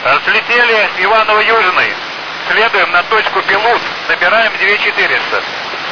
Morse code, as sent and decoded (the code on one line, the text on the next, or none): none